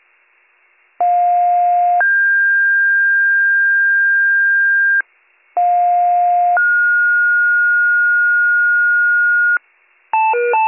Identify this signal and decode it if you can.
Two-tone page, heard as a one-second tone, then a three-second tone, about 710 then 1600 Hz, about 710 then 1430 Hz, with noise. It is heard from 1.0 to 9.6 s.